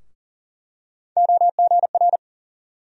Morse code, heard as a keyed tone, about 40 wpm, 700 Hz. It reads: OGR
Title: USA Call Sign Postfixes - 3 Letters 40wpm